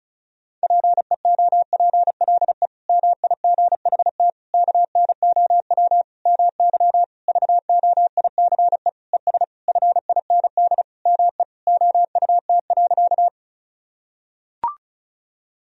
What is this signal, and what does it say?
Morse code, heard as a keyed tone, about 36 wpm, 700 Hz. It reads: PEOPLE MIGHT KNOW MY VOICE ES FIND ME OUT.
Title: Sentences from Top 500 Words and Abbreviations 35wpm